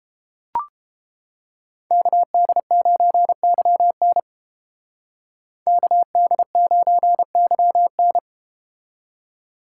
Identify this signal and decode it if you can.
Morse code, heard as a keyed tone, about 31 wpm, 700 Hz. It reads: KD9YN KD9YN